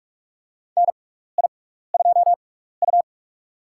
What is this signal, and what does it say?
Morse code, heard as a keyed tone, about 46 wpm, spaced as for 16 wpm, 700 Hz. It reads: NI2U
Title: USA Call Signs - 4 Characters - 6x Character Spacing - 45wpm